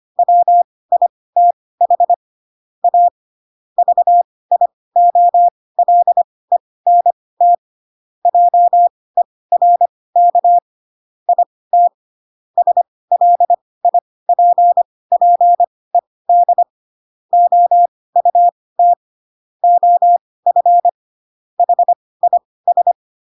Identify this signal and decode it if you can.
Morse code, heard as a keyed tone, about 25 wpm, spaced as for 18 wpm, 700 Hz. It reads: WITH A VIOLENT JERK IT SLIPPED OUT OF HIS
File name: Pinocchio / Pinocchio ch2 12 25 ik1hge